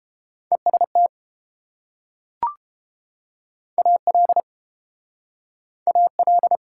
Morse code, heard as a keyed tone, about 29 wpm, 700 Hz. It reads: EST AL AL